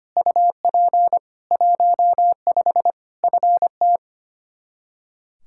Morse code, heard as a keyed tone, about 25 wpm, 700 Hz. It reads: UP 15 FT
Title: K1USN SST QSO Practice 25wpm